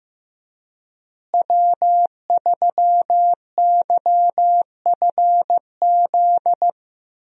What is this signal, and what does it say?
Morse code, heard as a keyed tone, about 15 wpm, 700 Hz. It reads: W3YFZ
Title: USA Call Signs - 5 Characters - Repeated 3x - 15wpm